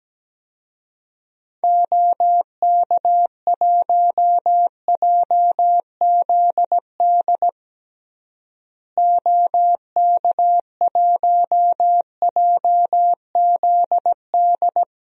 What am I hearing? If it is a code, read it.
Morse code, heard as a keyed tone, about 17 wpm, 700 Hz. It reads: OK1JZD OK1JZD
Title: DXCC-100 Call Signs - 6 Characters - Repeated 2x - 17wpm